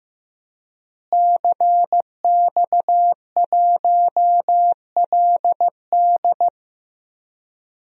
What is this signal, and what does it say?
Morse code, heard as a keyed tone, about 15 wpm, 700 Hz. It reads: CX1LD
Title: DXCC-100 Call Signs - 5 Characters - Repeated 3x - 15wpm